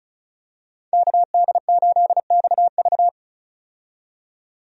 Morse code, heard as a keyed tone, about 36 wpm, 700 Hz. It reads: KD8XV
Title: USA Call Signs - 5 Characters - 35wpm